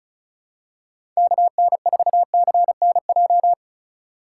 Morse code, heard as a keyed tone, about 36 wpm, 700 Hz. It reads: KN4CNJ
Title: USA Call Signs - 6 Characters - 3x Character Spacing - 35wpm